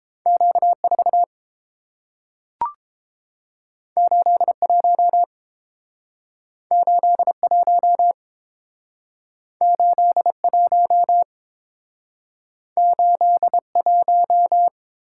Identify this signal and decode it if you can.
Morse code, heard as a keyed tone, about 30 wpm, 700 Hz. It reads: Q 4 8 1 8 1 8 1 8 1